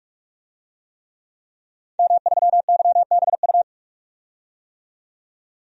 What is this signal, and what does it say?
Morse code, heard as a keyed tone, about 46 wpm, 700 Hz. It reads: M3YBU
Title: DXCC-100 Call Signs - 5 Characters - 45wpm